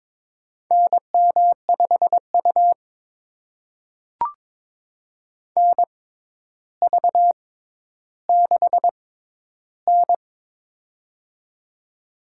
Morse code, heard as a keyed tone, about 22 wpm, spaced as for 11 wpm, 700 Hz. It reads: NM5U NV6N
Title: USA Call Signs - 4 Characters - 6x Character Spacing - 22wpm